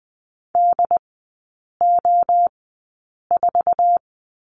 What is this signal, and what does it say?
Morse code, heard as a keyed tone, about 20 wpm, spaced as for 8 wpm, 700 Hz. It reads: DO4
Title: 067 Lesson 12e - Letter d - Three Characters 20wpm